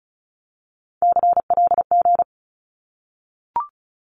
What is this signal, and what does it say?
Morse code, heard as a keyed tone, about 34 wpm, 700 Hz. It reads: CLG
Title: Abbreviations Repeated 3x - 35wpm